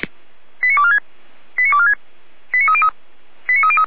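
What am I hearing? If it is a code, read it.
Five-tone selective call (ZVEI), heard as quick runs of five tones, tones of about 70 ms. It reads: 89246 89246 89303 89303
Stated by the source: ZVEI five-tone call received via SDR, recorded by feuerwehr-orlamuende fire brigade (f85360).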